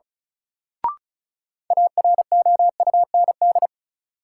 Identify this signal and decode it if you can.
Morse code, heard as a keyed tone, about 36 wpm, 700 Hz. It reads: AROUND
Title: Sentences from Top 600 Words - New Word Drill - 35wpm